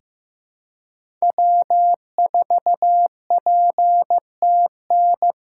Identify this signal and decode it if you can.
Morse code, heard as a keyed tone, about 15 wpm, 700 Hz. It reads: W4PTN
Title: USA Call Signs - 5 Characters - 8x Character Spacing - 15wpm